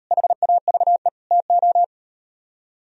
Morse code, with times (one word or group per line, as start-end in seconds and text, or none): 0.1-1.1 HAVE
1.3-1.8 TO